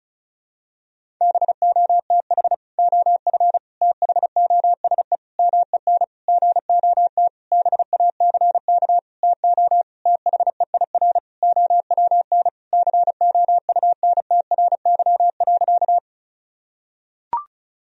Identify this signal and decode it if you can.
Morse code, heard as a keyed tone, about 34 wpm, 700 Hz. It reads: BOTH OF THOSE MEN GOT BACK TO THEIR OWN COUNTRY.
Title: Sentences from Top 1000 Words - Review Part 1 - 35wpm